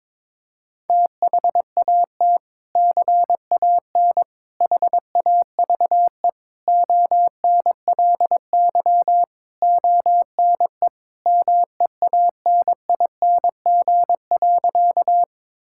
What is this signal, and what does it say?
Morse code, heard as a keyed tone, about 21 wpm, 700 Hz. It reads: THAT CAN HAVE ONLY ONE MEANING.
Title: Sentences from Top 1000 Words - Review Part 6 - 22wpm